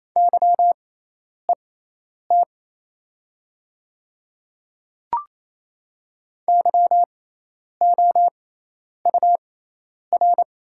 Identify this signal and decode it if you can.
Morse code, heard as a keyed tone, about 27 wpm, spaced as for 10 wpm, 700 Hz. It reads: YET YOUR
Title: 6x Character Spacing - Top 500 Words - 28wpm